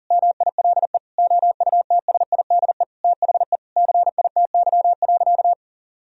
Morse code, heard as a keyed tone, about 40 wpm, 700 Hz. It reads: MILE OUTSIDE THE CITY.
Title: Sentences from Top 1000 Words - Review Part 4 - 40wpm